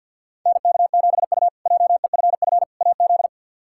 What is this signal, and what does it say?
Morse code, heard as a keyed tone, about 51 wpm, 700 Hz. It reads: NK7U JEFF AZ
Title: K1USN SST QSO Practice - 50wpm